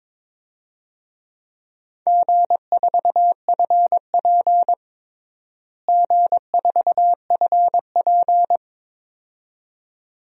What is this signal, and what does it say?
Morse code, heard as a keyed tone, about 22 wpm, 700 Hz. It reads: G4FP G4FP
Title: DXCC-100 Call Signs - 4 Characters - Repeated 2x - 22wpm